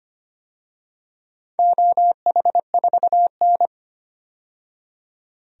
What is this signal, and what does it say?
Morse code, heard as a keyed tone, about 25 wpm, 700 Hz. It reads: OH4N
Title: DXCC-100 Call Signs - 4 Characters - 25wpm